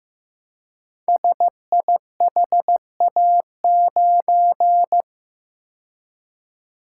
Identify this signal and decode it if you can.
Morse code, heard as a keyed tone, about 15 wpm, 700 Hz. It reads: SIHA9
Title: Code Groups - 5 Letter:Numbers - 15wpm